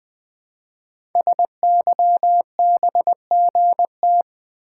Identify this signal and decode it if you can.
Morse code, heard as a keyed tone, about 21 wpm, 700 Hz. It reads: SYBGT
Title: Code Groups - 5 Letters - 20wpm